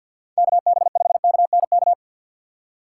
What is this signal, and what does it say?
Morse code, heard as a keyed tone, about 47 wpm, 700 Hz. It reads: KB5KNK
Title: USA Call Signs - 6 Characters - 4x Character Spacing - 50wpm